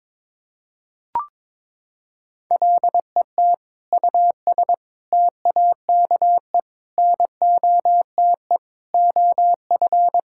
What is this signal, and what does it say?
Morse code, heard as a keyed tone, about 22 wpm, 700 Hz. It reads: LET US TAKE NOTE OF
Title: Sentences from Top 1600 Words 22wpm